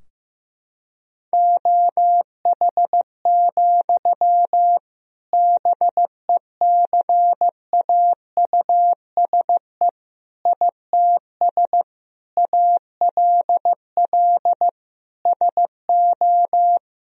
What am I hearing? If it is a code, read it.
Morse code, heard as a keyed tone, about 15 wpm, 700 Hz. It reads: OH, BECAUSE ITS ALL SO
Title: Sentences from Top 1000 Words - Review Part 3 - 15wpm